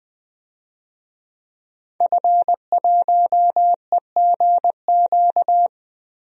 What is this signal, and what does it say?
Morse code, heard as a keyed tone, about 21 wpm, 700 Hz. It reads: F1EGQ
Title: DXCC-100 Call Signs - 5 Characters - Repeated 2x - 20wpm